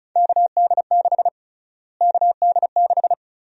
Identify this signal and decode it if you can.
Morse code, heard as a keyed tone, about 34 wpm, 700 Hz. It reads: KD6 KD6
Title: USA Call Sign Prefixes - Repeated 2x - 35wpm